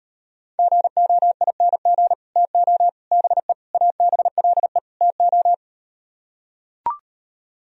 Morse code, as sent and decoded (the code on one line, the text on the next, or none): --. --- .. -. --. / - --- / -... / .- -... .-.. / - ---
GOING TO BE ABLE TO